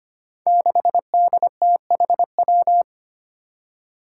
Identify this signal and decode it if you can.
Morse code, heard as a keyed tone, about 25 wpm, 700 Hz. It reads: 6DTHW
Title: Five Challenging Letter-Number Combinations 25wpm